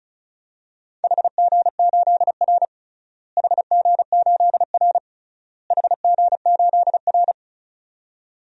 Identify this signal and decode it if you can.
Morse code, heard as a keyed tone, about 36 wpm, 700 Hz. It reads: HG8R HG8R HG8R
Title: DXCC-100 Call Signs - 4 Characters - Repeated 3x - 35wpm